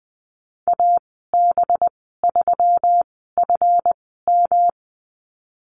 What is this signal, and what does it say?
Morse code, heard as a keyed tone, about 19 wpm, spaced as for 14 wpm, 700 Hz. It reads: AB3FM
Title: 193 Lesson 31g - Letter v - Call Signs 20wpm